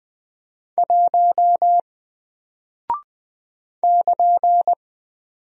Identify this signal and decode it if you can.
Morse code, heard as a keyed tone, about 20 wpm, 700 Hz. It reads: 1 <KN>
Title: Single Character (Letter:Number:Punctuation:Prosign) 20wpm